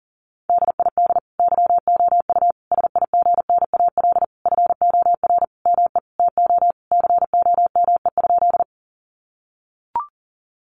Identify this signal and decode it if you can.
Morse code, heard as a keyed tone, about 40 wpm, 700 Hz. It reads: DID YOU SIGNAL FOR ME TO COME?
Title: Sentences from Top 1400 Words 40wpm